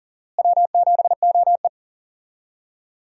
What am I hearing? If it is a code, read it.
Morse code, heard as a keyed tone, about 40 wpm, 700 Hz. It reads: W7OE
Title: USA Call Signs - 4 Characters - Repeated 3x - 40wpm